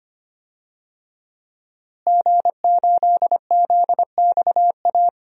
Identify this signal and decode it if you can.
Morse code, heard as a keyed tone, about 25 wpm, 700 Hz. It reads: G8ZXA
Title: DXCC-100 Call Signs - 5 Characters - Repeated 3x - 25wpm